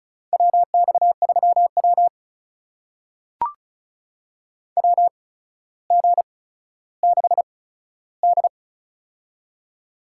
Morse code, heard as a keyed tone, about 36 wpm, spaced as for 15 wpm, 700 Hz. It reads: WX3W WG6D
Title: USA Call Signs - 4 Characters - 8x Character Spacing - 35wpm